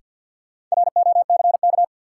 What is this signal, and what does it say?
Morse code, heard as a keyed tone, about 53 wpm, 700 Hz. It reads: ROCK
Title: Sentences from Top 400 Words - New Word Drill - 50wpm